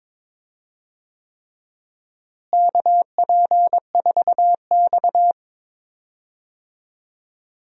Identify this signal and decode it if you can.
Morse code, heard as a keyed tone, about 22 wpm, 700 Hz. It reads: KP4X
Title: DXCC-100 Call Signs - 4 Characters - Repeated 3x - 22wpm